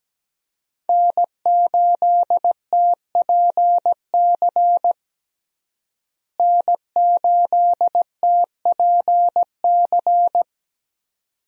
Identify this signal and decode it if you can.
Morse code, heard as a keyed tone, about 17 wpm, 700 Hz. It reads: N8TPC N8TPC
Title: USA Call Signs - 5 Characters - Repeated 2x - 17wpm